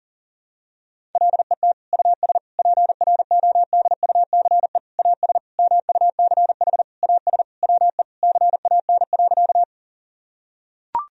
Morse code, heard as a keyed tone, about 40 wpm, 700 Hz. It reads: LET US PRODUCE AS MUCH AS WE CAN.
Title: Sentences from Top 1000 Words - Review Part 3 - 40wpm